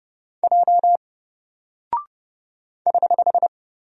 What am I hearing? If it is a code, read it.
Morse code, heard as a keyed tone, about 32 wpm, 700 Hz. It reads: J <HH>
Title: Single Character (Letter:Number:Punctuation:Prosign) Alt 30wpm